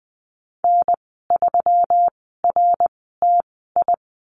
Morse code, heard as a keyed tone, about 20 wpm, spaced as for 14 wpm, 700 Hz. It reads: N3RTI
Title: 140 Lesson 23g - Letter f - Call Signs 20wpm